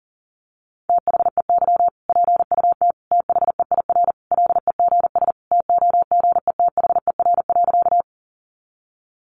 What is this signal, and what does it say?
Morse code, heard as a keyed tone, about 40 wpm, 700 Hz. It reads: THEY PUT THEIR LEGS TOGETHER.
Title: Easy Phrases 40wpm